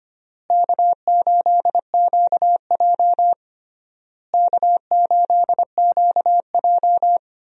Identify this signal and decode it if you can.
Morse code, heard as a keyed tone, about 25 wpm, 700 Hz. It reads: K8QJ K8QJ
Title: USA Call Signs - 4 Characters - Repeated 3x - 25wpm